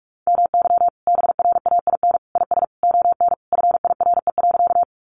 Morse code, heard as a keyed tone, about 44 wpm, 700 Hz. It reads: MY BRAIN IS ON FIRE.
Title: Sentences from Top 1100 Words 45wpm